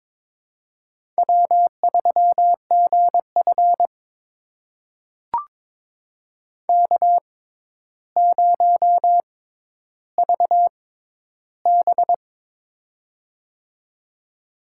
Morse code, heard as a keyed tone, about 22 wpm, spaced as for 11 wpm, 700 Hz. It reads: W3GF K0VB
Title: USA Call Signs - 4 Characters - 6x Character Spacing - 22wpm